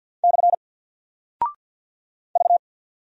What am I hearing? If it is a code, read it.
Morse code, heard as a keyed tone, about 49 wpm, 700 Hz. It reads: / V
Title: Single Character (Letter:Number:Punctuation:Prosign) Alt 50wpm